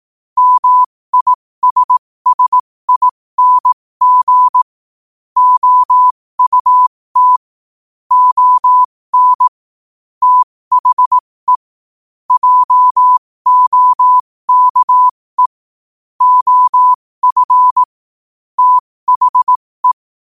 Morse code, heard as a keyed tone, about 18 wpm, spaced as for 15 wpm, 1000 Hz. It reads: MISSING OUT ON THE JOKE OF THE
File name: 15wpm